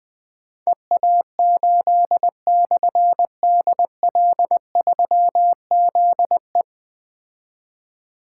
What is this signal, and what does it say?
Morse code, heard as a keyed tone, about 21 wpm, 700 Hz. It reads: EA8/DL3ZE
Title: Difficult Call Signs 20wpm